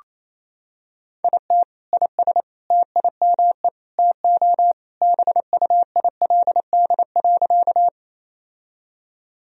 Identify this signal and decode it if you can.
Morse code, heard as a keyed tone, about 28 wpm, 700 Hz. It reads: IT IS TIME TO BUILD.
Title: Sentences from Top 1000 Words - Review Part 1 - 28wpm